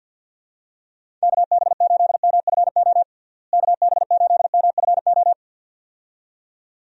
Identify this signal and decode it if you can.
Morse code, heard as a keyed tone, about 51 wpm, 700 Hz. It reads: KB8MFO KB8MFO
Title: USA Call Signs - 6 Characters - Repeated 2x - 50wpm